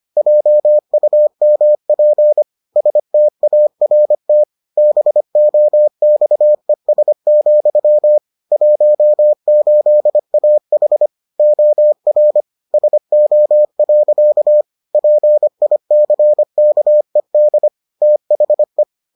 Morse code, heard as a keyed tone, about 25 wpm, 600 Hz. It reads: JUMP START BOXES, 18AH OR SO. PICKED THE